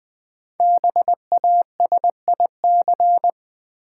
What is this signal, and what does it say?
Morse code, heard as a keyed tone, about 20 wpm, 700 Hz. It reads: BASIC
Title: Sentences from Top 1000 Words - Review Part 3 - New Word Drill - 20wpm